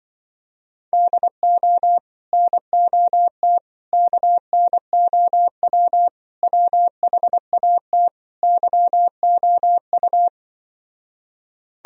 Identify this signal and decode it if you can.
Morse code, heard as a keyed tone, about 24 wpm, 700 Hz. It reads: DO NOT KNOW WHAT YOU
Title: Speed-Racing - Sets of 5 Words 20wpm